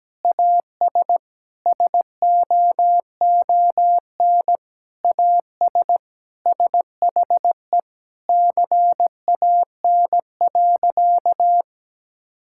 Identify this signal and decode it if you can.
Morse code, heard as a keyed tone, about 17 wpm, 700 Hz. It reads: AS SOON AS SHE CAN.
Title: Sentences from Top 1000 Words - Review Part 5 - 17wpm